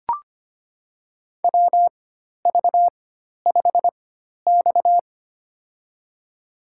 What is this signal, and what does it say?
Morse code, heard as a keyed tone, about 25 wpm, spaced as for 12 wpm, 700 Hz. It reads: WV5X
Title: USA Call Signs - 4 Characters - 4x Character Spacing - 25wpm